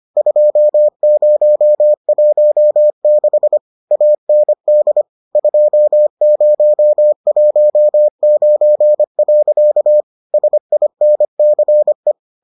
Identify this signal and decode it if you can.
Morse code, heard as a keyed tone, about 25 wpm, 600 Hz. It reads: 2016 AND 2019. SINCE